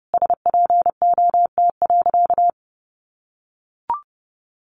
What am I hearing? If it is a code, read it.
Morse code, heard as a keyed tone, about 30 wpm, 700 Hz. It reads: SPOT.
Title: Sentences from Top 1200 Words 30wpm